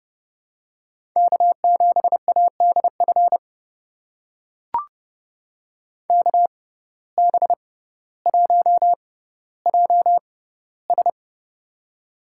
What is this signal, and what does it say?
Morse code, heard as a keyed tone, about 30 wpm, spaced as for 15 wpm, 700 Hz. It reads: K7ADF KB1JS
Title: USA Call Signs - 5 Characters - 6x Character Spacing - 30wpm